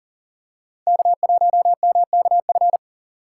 Morse code, heard as a keyed tone, about 40 wpm, 700 Hz. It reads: K1MKF
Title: USA Call Signs - 5 Characters - 3x Character Spacing - 40wpm